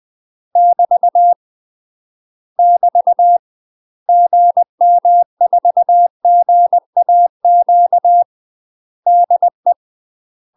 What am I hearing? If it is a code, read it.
Morse code, heard as a keyed tone, about 20 wpm, 700 Hz. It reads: <BT> <BT> GM4GAQ DE
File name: 20WPM